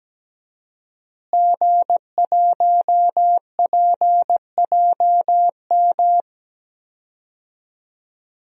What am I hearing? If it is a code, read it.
Morse code, heard as a keyed tone, about 17 wpm, 700 Hz. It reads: G1PJM
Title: DXCC-100 Call Signs - 5 Characters - Repeated 3x - 17wpm